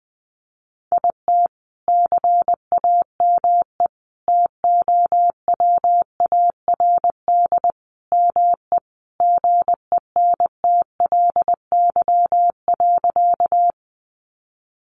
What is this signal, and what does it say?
Morse code, heard as a keyed tone, about 20 wpm, 700 Hz. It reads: IT CAME TOWARD ME GENTLY.
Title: Sentences from Top 1500 Words 20wpm